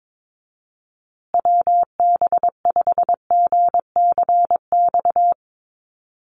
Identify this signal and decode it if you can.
Morse code, heard as a keyed tone, about 22 wpm, 700 Hz. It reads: WB5GCX
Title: USA Call Signs Repeated 2x 22wpm